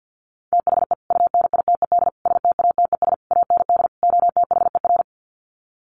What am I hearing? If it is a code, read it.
Morse code, heard as a keyed tone, about 50 wpm, 700 Hz. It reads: THE UNITED STATES AND OTHER
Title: Sets of 5 Words Encore 50wpm